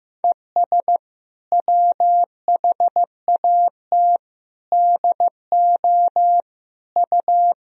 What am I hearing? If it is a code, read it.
Morse code, heard as a keyed tone, about 15 wpm, 700 Hz. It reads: ES WHAT DO U
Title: Sentences from Top 100 Words and Abbreviations 15wpm